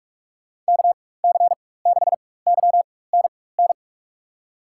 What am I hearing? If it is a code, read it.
Morse code, heard as a keyed tone, about 45 wpm, spaced as for 21 wpm, 700 Hz. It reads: KC6YNN
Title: USA Call Signs - 4x Character Spacing - 45wpm